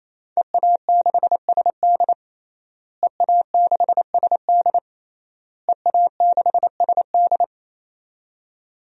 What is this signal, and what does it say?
Morse code, heard as a keyed tone, about 28 wpm, 700 Hz. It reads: EA6SD EA6SD EA6SD
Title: DXCC-100 Call Signs - 5 Characters - Repeated 3x - 28wpm